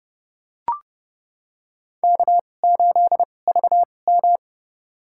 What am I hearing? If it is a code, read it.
Morse code, heard as a keyed tone, about 30 wpm, spaced as for 22 wpm, 700 Hz. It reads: K8VM
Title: USA Call Signs - 2x Character Spacing - 30wpm